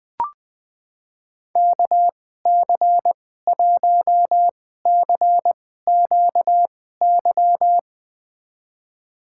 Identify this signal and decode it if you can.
Morse code, heard as a keyed tone, about 20 wpm, spaced as for 14 wpm, 700 Hz. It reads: KC1CQY
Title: USA Call Signs - 6 Characters - 2x Character Spacing - 20wpm